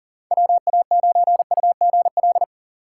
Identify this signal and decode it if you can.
Morse code, heard as a keyed tone, about 40 wpm, 700 Hz. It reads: WA9UGL
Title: USA Call Signs - 6 Characters - 4x Character Spacing - 40wpm